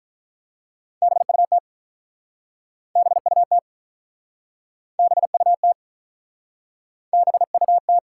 Morse code, heard as a keyed tone, about 46 wpm, 700 Hz. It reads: BUT BUT BUT BUT